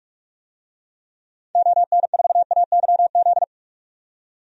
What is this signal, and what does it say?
Morse code, heard as a keyed tone, about 46 wpm, 700 Hz. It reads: ON4AYZ
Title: DXCC-100 Call Signs - 6 Characters - 45wpm